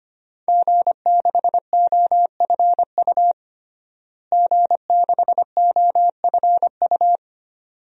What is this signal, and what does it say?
Morse code, heard as a keyed tone, about 25 wpm, 700 Hz. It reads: G6OFU G6OFU
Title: DXCC-100 Call Signs - 5 Characters - Repeated 2x - 25wpm